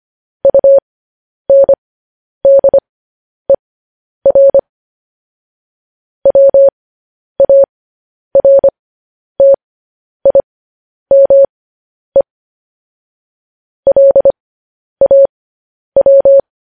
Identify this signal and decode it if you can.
Morse code, heard as a keyed tone, about 25 wpm, spaced as for 10 wpm, 550 Hz. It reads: UNDER WARTIME LAW
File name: podcast / News headlines-1x-25x10